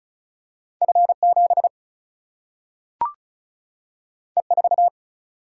Morse code, heard as a keyed tone, about 36 wpm, 700 Hz. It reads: F 7 E 4
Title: Single-Letter Single-Number 35wpm